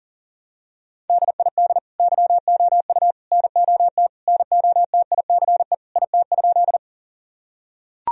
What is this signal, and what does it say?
Morse code, heard as a keyed tone, about 40 wpm, 700 Hz. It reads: DID YOU NOT NOTICE IT?
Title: Sentences from Top 1000 Words - Review Part 2 - 40wpm